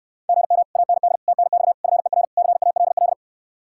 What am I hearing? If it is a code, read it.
Morse code, heard as a keyed tone, about 69 wpm, 700 Hz. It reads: GM AND TNX FER CALL